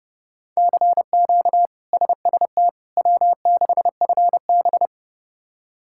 Morse code, heard as a keyed tone, about 30 wpm, 700 Hz. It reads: CQ SST W6FB
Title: K1USN SST QSO Practice - 30wpm